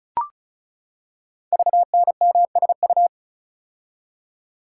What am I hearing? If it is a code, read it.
Morse code, heard as a keyed tone, about 36 wpm, 690 Hz. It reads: VNMSU
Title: Code Groups - 5 Letters - 35wpm